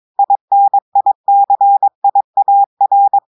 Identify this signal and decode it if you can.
Morse code, heard as a keyed tone, about 23 wpm, 800 Hz. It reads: INICIAR
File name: audios / portugais / portugais texte en clair 2 vitesse 22 mots minutes